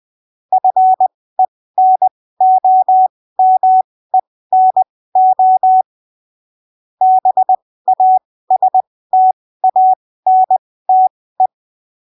Morse code, heard as a keyed tone, about 20 wpm, spaced as for 15 wpm, 750 Hz. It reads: FENOMENO BASTANTE
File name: arquivos / texto / texto 24 20ppm